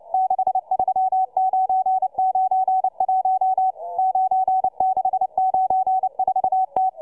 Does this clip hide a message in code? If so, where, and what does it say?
Yes: Morse code from 0.1 to 6.9 s, heard as a keyed tone, about 29 wpm, 750 Hz, with radio noise. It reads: 63991 9694T